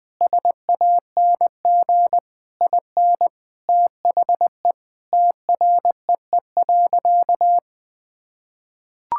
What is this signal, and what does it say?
Morse code, heard as a keyed tone, about 20 wpm, 700 Hz. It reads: SANG IN THE TREE.
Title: Sentences with 2-Word Phrases - 20wpm